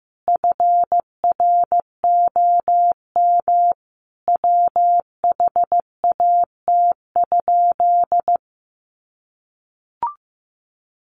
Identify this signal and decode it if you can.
Morse code, heard as a keyed tone, about 14 wpm, 700 Hz. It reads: FROM WHAT?
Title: Sentences from Top 1100 Words 15wpm